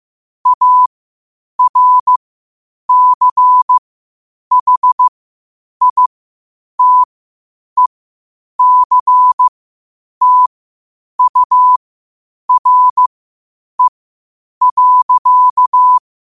Morse code, heard as a keyed tone, about 15 wpm, spaced as for 8 wpm, 1000 Hz. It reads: ARCHITECTURE.
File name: qotdfiles / 8wpm